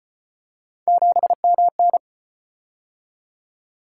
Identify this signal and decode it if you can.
Morse code, heard as a keyed tone, about 35 wpm, 700 Hz. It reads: 7MN